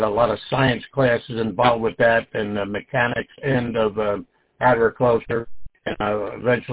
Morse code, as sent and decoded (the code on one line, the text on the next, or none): none